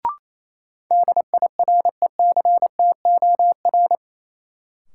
Morse code, heard as a keyed tone, about 27 wpm, 700 Hz. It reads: DIRECTOR
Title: Sentences from Top 1300 Words - New Word Drill Alt - 28wpm